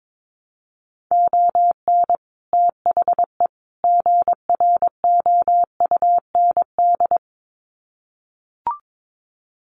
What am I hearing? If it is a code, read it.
Morse code, heard as a keyed tone, about 22 wpm, 700 Hz. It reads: ON THE GROUND
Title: Sets of 3 Words 22wpm